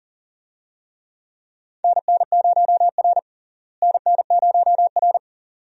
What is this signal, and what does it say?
Morse code, heard as a keyed tone, about 40 wpm, 700 Hz. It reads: NN0R NN0R
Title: USA Call Signs - 4 Characters - Repeated 2x - 40wpm